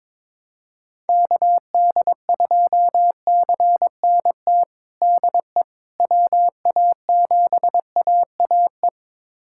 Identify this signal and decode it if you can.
Morse code, heard as a keyed tone, about 22 wpm, 700 Hz. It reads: KD2CNT DE WA7AAE
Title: USA Callsign DE Callsign - 6 Characters - 22wpm